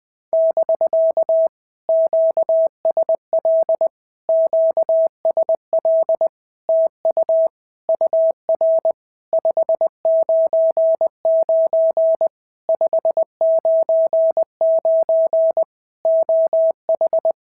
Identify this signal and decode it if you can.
Morse code, heard as a keyed tone, about 18 wpm, 650 Hz. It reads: <BK> QSL QSL TU UR 599 599 OH
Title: POTA QSO Practice - 20wpm